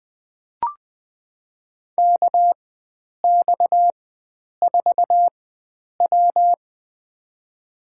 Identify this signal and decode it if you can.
Morse code, heard as a keyed tone, about 21 wpm, spaced as for 9 wpm, 700 Hz. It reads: KX4W
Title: USA Call Signs - 4 Characters - 4x Character Spacing - 20wpm